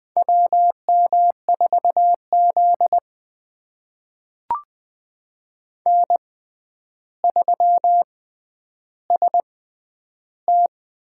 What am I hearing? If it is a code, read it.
Morse code, heard as a keyed tone, about 20 wpm, spaced as for 10 wpm, 700 Hz. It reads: WM4Z N3ST
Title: USA Call Signs - 6x Character Spacing - 20wpm